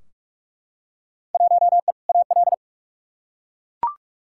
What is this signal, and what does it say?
Morse code, heard as a keyed tone, about 43 wpm, 700 Hz. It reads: 1E AL
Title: Field Day Exchanges - 45wpm